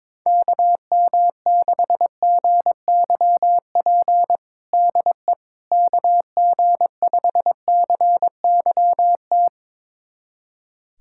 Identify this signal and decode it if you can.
Morse code, heard as a keyed tone, about 22 wpm, 700 Hz. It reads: KM6GYP DE KG5CYT
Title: USA Callsign DE Callsign - 6 Characters - 22wpm